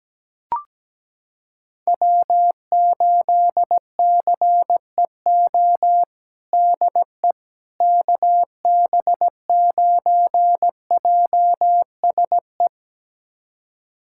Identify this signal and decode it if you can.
Morse code, heard as a keyed tone, about 17 wpm, 700 Hz. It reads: W8CEO DE KB9JSE
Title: USA Callsign DE Callsign 17wpm